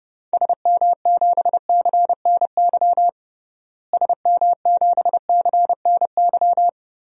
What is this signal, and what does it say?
Morse code, heard as a keyed tone, about 30 wpm, 700 Hz. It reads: SM7CNY SM7CNY
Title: DXCC-100 Call Signs - 6 Characters - Repeated 3x - 30wpm